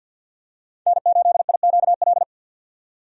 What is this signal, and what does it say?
Morse code, heard as a keyed tone, about 51 wpm, 700 Hz. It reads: N8IQL
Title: USA Call Signs - 5 Characters - 50wpm